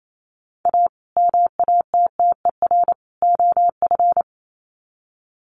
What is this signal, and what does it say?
Morse code, heard as a keyed tone, about 28 wpm, 700 Hz. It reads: A MATTER OF
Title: Sets of 3 Words 28wpm